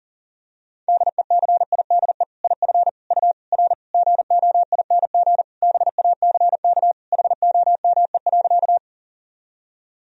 Morse code, heard as a keyed tone, about 40 wpm, 700 Hz. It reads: DECIDE IF U R GOING BACK HOME.
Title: Sentences from Top 500 Words and Abbreviations 40wpm